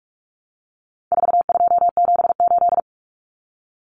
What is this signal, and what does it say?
Morse code, heard as a keyed tone, about 44 wpm, 700 Hz. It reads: 4278
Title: Numbers - 4-Digit 45wpm